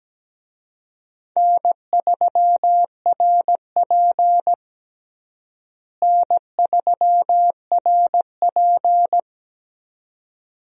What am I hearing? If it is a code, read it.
Morse code, heard as a keyed tone, about 17 wpm, 700 Hz. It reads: N3RP N3RP
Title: USA Call Signs - 4 Characters - Repeated 2x - 17wpm